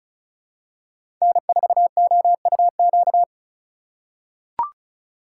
Morse code, heard as a keyed tone, about 36 wpm, 700 Hz. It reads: N4OUQ
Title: USA Call Signs - 5 Characters - 2x Character Spacing - 35wpm